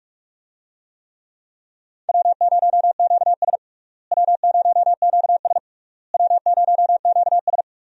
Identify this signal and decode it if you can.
Morse code, heard as a keyed tone, about 47 wpm, 700 Hz. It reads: W0QS W0QS W0QS
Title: USA Call Signs - 4 Characters - Repeated 3x - 45wpm